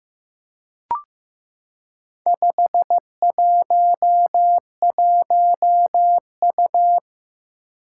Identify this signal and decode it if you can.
Morse code, heard as a keyed tone, about 15 wpm, 700 Hz. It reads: 511U
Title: Four Challenging Letter-Number Combinations 15wpm